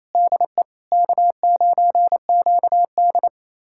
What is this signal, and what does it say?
Morse code, heard as a keyed tone, about 28 wpm, 700 Hz. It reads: DE K9QD
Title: USA Callsign DE Callsign - 4 Characters - 28wpm